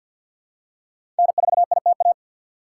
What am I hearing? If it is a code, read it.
Morse code, heard as a keyed tone, about 51 wpm, 700 Hz. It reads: N4ITA
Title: USA Call Signs - 5 Characters - 3x Character Spacing - 50wpm